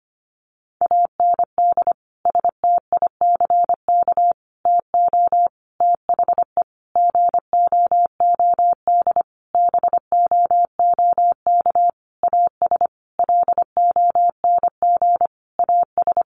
Morse code, heard as a keyed tone, about 25 wpm, 700 Hz. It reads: AND STICK TO THE GOOD BOOK AS LONG AS
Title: Sentences from Top 1200 Words 25wpm